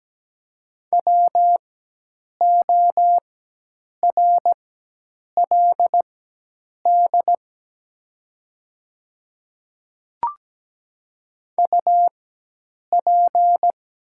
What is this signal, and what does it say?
Morse code, heard as a keyed tone, about 17 wpm, spaced as for 8 wpm, 690 Hz. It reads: WORLD UP
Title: 4x Character Spacing - Top 200 Words - 17wpm